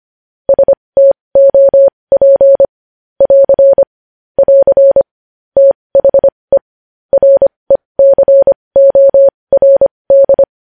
Morse code, heard as a keyed tone, about 25 wpm, spaced as for 20 wpm, 550 Hz. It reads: STOP <AR> <AR> THE RECORD
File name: podcast / News headlines-1x-25x20